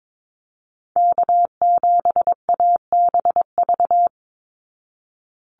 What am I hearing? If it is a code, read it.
Morse code, heard as a keyed tone, about 22 wpm, 700 Hz. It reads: K7ABV
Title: Call Signs Repeated 3x - 22wpm